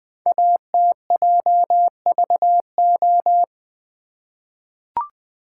Morse code, heard as a keyed tone, about 20 wpm, 700 Hz. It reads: ATJVO
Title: Code Groups - 5 Letters - 20wpm